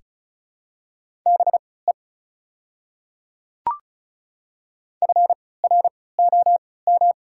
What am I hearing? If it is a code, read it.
Morse code, heard as a keyed tone, about 36 wpm, spaced as for 20 wpm, 700 Hz. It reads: BE FROM
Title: 3x Character Spacing - Top 300 Words - 35wpm